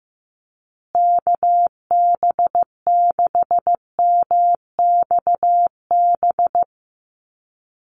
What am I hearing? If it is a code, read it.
Morse code, heard as a keyed tone, about 15 wpm, 700 Hz. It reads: KB6MXB
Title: USA Call Signs Repeated 2x 15wpm